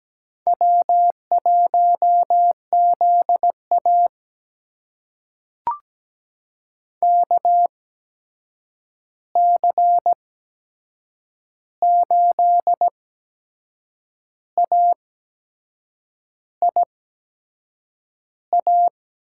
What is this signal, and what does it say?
Morse code, heard as a keyed tone, about 17 wpm, spaced as for 6 wpm, 700 Hz. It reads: W1ZA KC8AIA
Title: USA Call Signs - 8x Character Spacing - 17wpm